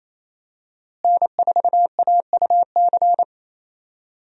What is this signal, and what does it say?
Morse code, heard as a keyed tone, about 28 wpm, 700 Hz. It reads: N4AUC
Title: USA Call Signs - 5 Characters - Repeated 2x - 28wpm